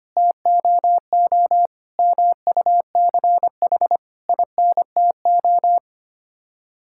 Morse code, heard as a keyed tone, about 25 wpm, 700 Hz. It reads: TOO MUCH INTO
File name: Sets of 9 Words - 25wpm